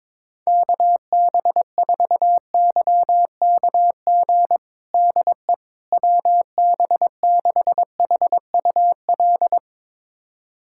Morse code, heard as a keyed tone, about 22 wpm, 700 Hz. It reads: KB4YKG DE WB6HUL
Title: USA Callsign DE Callsign 22wpm